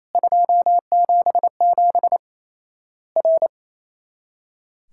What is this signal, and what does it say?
Morse code, heard as a keyed tone, about 28 wpm, 700 Hz. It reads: 277 R
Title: Worked All Europe DX Contest - QTC Exchange - 28wpm